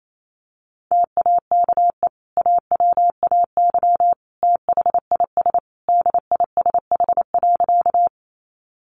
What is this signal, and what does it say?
Morse code, heard as a keyed tone, about 28 wpm, 700 Hz. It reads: TAKE AWAY THIS DISH.
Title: Sentences from Top 1300 Words 28wpm